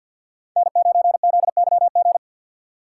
Morse code, heard as a keyed tone, about 53 wpm, 700 Hz. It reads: N9ZYG
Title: USA Call Signs - 6x Character Spacing - 50wpm